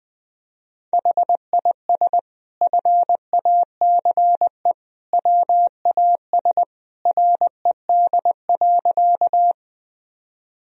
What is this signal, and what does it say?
Morse code, heard as a keyed tone, about 20 wpm, 700 Hz. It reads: HIS FACE WAS RED.
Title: Sentences from Top 1000 Words - Review Part 2 - 20wpm